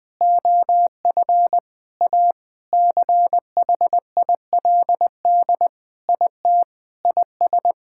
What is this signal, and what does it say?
Morse code, heard as a keyed tone, about 20 wpm, 700 Hz. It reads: OF A CHILD IT IS
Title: Sentences with 5-Word Phrases - 20wpm